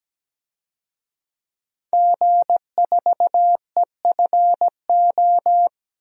Morse code, heard as a keyed tone, about 17 wpm, 700 Hz. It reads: G4EFO
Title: DXCC-100 Call Signs - 5 Characters - 17wpm